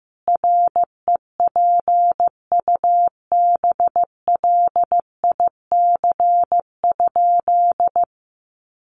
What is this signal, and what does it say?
Morse code, heard as a keyed tone, about 15 wpm, 700 Hz. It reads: REPUBLIC?
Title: Sentences from Top 1500 Words 15wpm